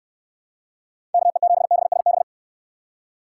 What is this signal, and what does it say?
Morse code, heard as a keyed tone, about 62 wpm, 700 Hz. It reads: KE7BSB